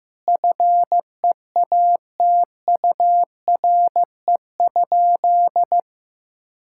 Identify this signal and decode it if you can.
Morse code, heard as a keyed tone, about 15 wpm, 700 Hz. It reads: FEATURE?
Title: Sentences from Top 1000 Words - Review Part 5 - 15wpm